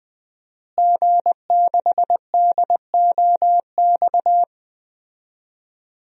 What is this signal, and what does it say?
Morse code, heard as a keyed tone, about 20 wpm, 700 Hz. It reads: G6DOX
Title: DXCC-100 Call Signs - 5 Characters - Repeated 3x - 20wpm